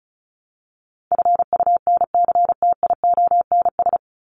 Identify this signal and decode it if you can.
Morse code, heard as a keyed tone, about 34 wpm, 700 Hz. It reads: FUNCTIONS
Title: ES:S Suffix - Top 100 Words with Base Words 35wpm